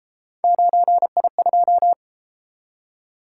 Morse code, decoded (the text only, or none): 9I2